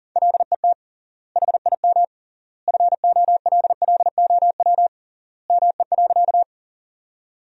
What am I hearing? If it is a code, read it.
Morse code, heard as a keyed tone, about 40 wpm, 700 Hz. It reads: LET HIM FOLLOW ME.
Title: 3x Word-Spacing - Sentences from Top 300 Words - 40wpm